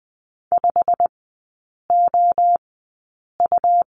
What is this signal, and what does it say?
Morse code, heard as a keyed tone, about 20 wpm, spaced as for 8 wpm, 690 Hz. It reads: 5OU
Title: 159 Lesson 26e - Letter g - Three Characters 20wpm